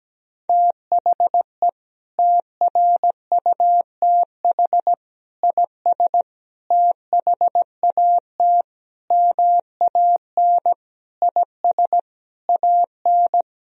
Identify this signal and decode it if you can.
Morse code, heard as a keyed tone, about 17 wpm, 700 Hz. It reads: THE TRUTH IS THAT MAN IS AN
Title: Sentences from Top 1000 Words - Review Part 1 - 17wpm